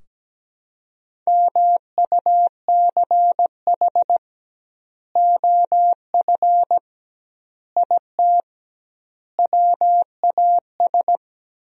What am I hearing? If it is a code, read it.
Morse code, heard as a keyed tone, about 17 wpm, 700 Hz. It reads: MUCH OF IT WAS
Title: 2x Word-Spacing - Sentences from Top 400 Words - 17wpm